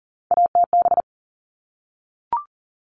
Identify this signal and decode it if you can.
Morse code, heard as a keyed tone, about 40 wpm, 700 Hz. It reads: ATB
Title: All Cut Numbers - 3-Digit 40wpm